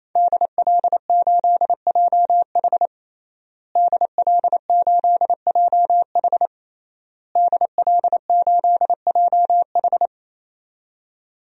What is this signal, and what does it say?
Morse code, heard as a keyed tone, about 28 wpm, 700 Hz. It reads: DL8JH DL8JH DL8JH
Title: DXCC-100 Call Signs - 5 Characters - Repeated 3x - 28wpm